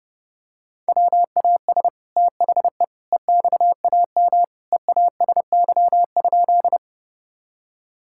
Morse code, heard as a keyed tone, about 30 wpm, 700 Hz. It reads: WAS THE EXAM EASY?
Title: Sentences from Top 1000 Words - Review Part 5 - 30wpm